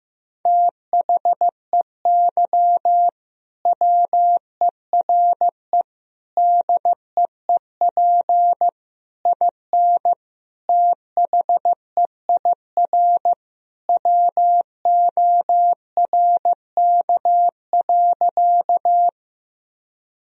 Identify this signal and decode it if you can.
Morse code, heard as a keyed tone, about 16 wpm, 700 Hz. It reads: THEY WERE DEEP IN THEIR WORK.
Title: Sentences from Top 1000 Words - Review Part 3 - 15wpm